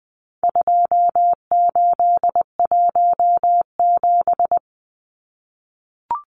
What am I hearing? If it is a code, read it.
Morse code, heard as a keyed tone, about 20 wpm, 700 Hz. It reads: 2817
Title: Numbers - 4-Digit 20wpm